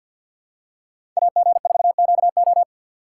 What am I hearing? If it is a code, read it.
Morse code, heard as a keyed tone, about 47 wpm, 690 Hz. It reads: AG4QO
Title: USA Call Signs - 5 Characters - 6x Character Spacing - 50wpm